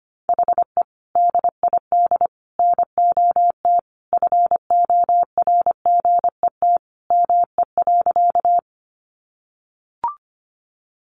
Morse code, heard as a keyed tone, about 25 wpm, 700 Hz. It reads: HE DID NOT FORGET ME.
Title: Sentences from Top 1200 Words 25wpm